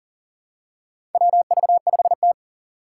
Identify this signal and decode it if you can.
Morse code, heard as a keyed tone, about 38 wpm, 700 Hz. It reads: WV5T
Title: Four Challenging Letter-Number Combinations 40wpm